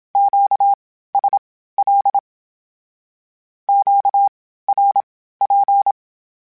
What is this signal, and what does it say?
Morse code, heard as a keyed tone, about 27 wpm, spaced as for 15 wpm, 800 Hz. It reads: QSL QRP
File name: (spaced) / Q-Calls Spaced-mp3s / Q Calls Twice sp @25